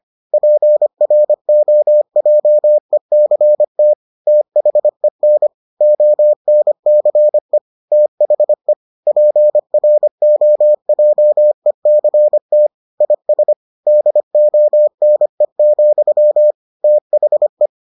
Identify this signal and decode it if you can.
Morse code, heard as a keyed tone, about 25 wpm, 600 Hz. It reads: PROJECT THEN ONCE THE PROJECT IS DONE, THE